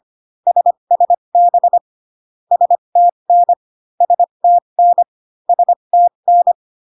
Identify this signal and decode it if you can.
Morse code, heard as a keyed tone, about 25 wpm, spaced as for 22 wpm, 700 Hz. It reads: SSB STN STN STN
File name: CW common abbreviations alphabetical repeated 3 times 25 18wpm